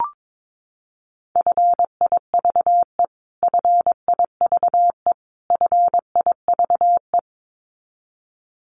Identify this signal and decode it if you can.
Morse code, heard as a keyed tone, about 22 wpm, 700 Hz. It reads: FIVE FIVE FIVE
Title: Top 500 Words Repeated 22wpm